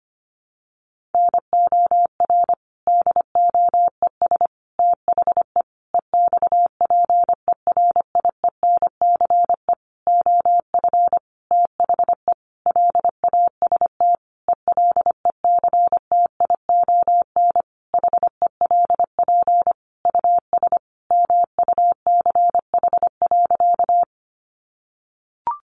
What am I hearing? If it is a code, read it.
Morse code, heard as a keyed tone, about 25 wpm, 700 Hz. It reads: NOR DOES THE EXPERIENCE OF THE LAST ELECTION HELP US MUCH.
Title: Sentences from Top 1500 Words 25wpm